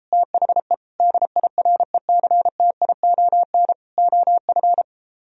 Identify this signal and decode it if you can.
Morse code, heard as a keyed tone, about 33 wpm, 700 Hz. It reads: THE DIRECTION OF